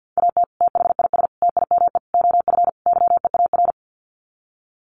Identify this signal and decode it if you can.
Morse code, heard as a keyed tone, about 50 wpm, 700 Hz. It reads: AT THIS TIME OF YEAR
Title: Sets of 5 Words Encore2 50wpm